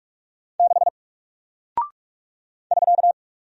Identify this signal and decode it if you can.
Morse code, heard as a keyed tone, about 46 wpm, 700 Hz. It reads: 6 <SK>
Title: Single Character (Letter:Number:Punctuation:Prosign) Alt 45wpm